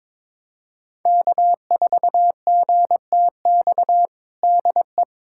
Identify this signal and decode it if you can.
Morse code, heard as a keyed tone, about 21 wpm, 700 Hz. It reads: K4GTX DE
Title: USA Callsign DE Callsign 22wpm